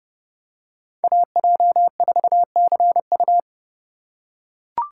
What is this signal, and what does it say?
Morse code, heard as a keyed tone, about 29 wpm, 700 Hz. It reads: AJ4CU